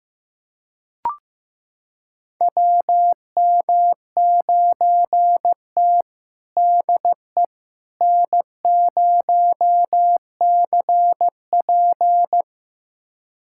Morse code, heard as a keyed tone, about 15 wpm, 700 Hz. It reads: WM9T DE N0CP
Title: USA Callsign DE Callsign - 4 Characters - 15wpm